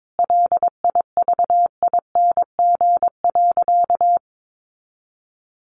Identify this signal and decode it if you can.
Morse code, heard as a keyed tone, about 22 wpm, 700 Hz. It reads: LIVING.
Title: Sentences from Top 1500 Words 22wpm